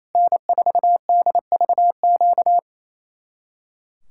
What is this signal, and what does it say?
Morse code, heard as a keyed tone, about 34 wpm, 700 Hz. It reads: N4DVQ
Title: Speed-Racing Common QSO Phrases Encore - 28wpm